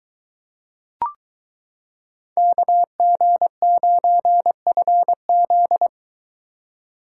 Morse code, heard as a keyed tone, about 23 wpm, 700 Hz. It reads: KG9FZ